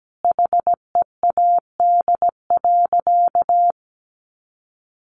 Morse code, heard as a keyed tone, about 17 wpm, 700 Hz. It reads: HEAD.
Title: Sentences from Top 1400 Words 17wpm